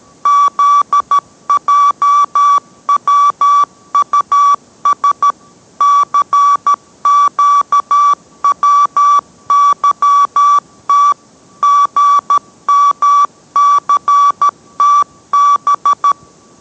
Morse code, heard as a keyed tone, about 13 wpm, 1150 Hz. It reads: ZJWUS CQWYT GMCTB